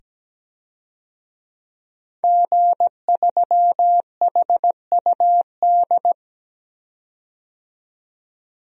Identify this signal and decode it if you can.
Morse code, heard as a keyed tone, about 17 wpm, 700 Hz. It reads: G3HUD
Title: DXCC-100 Call Signs - 5 Characters - 17wpm